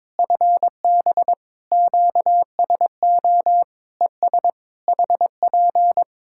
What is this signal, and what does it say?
Morse code, heard as a keyed tone, about 22 wpm, 700 Hz. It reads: FB QSO ES HP
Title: Practice QSO - Exchange 4 - Ending - 22wpm